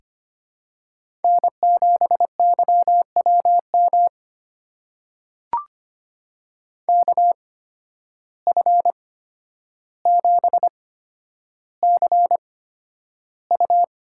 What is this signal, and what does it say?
Morse code, heard as a keyed tone, about 25 wpm, spaced as for 11 wpm, 700 Hz. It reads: N7YWM KF7CU
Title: USA Call Signs - 5 Characters - 8x Character Spacing - 25wpm